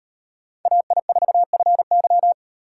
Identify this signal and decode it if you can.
Morse code, heard as a keyed tone, about 37 wpm, 700 Hz. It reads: AI4FY